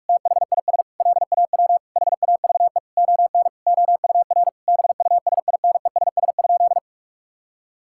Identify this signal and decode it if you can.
Morse code, heard as a keyed tone, about 41 wpm, 700 Hz. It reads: THIS LAW HAVE ON OUR BUSINESS?
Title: Sentences from Top 1000 Words - Review Part 6 - 45wpm